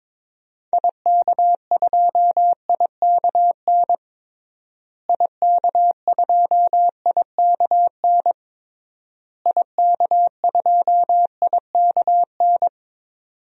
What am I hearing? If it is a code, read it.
Morse code, heard as a keyed tone, about 22 wpm, 700 Hz. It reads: IK2IKN IK2IKN IK2IKN
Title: DXCC-100 Call Signs - 6 Characters - Repeated 3x - 22wpm